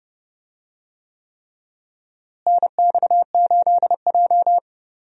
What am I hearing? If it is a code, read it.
Morse code, heard as a keyed tone, about 30 wpm, 700 Hz. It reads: NX8J
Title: USA Call Signs - 4 Characters - Repeated 3x - 30wpm